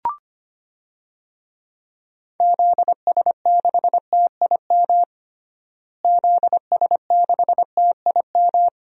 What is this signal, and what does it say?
Morse code, heard as a keyed tone, about 25 wpm, 700 Hz. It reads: ZS6TIM ZS6TIM
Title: DXCC-100 Call Signs - 6 Characters - Repeated 2x - 25wpm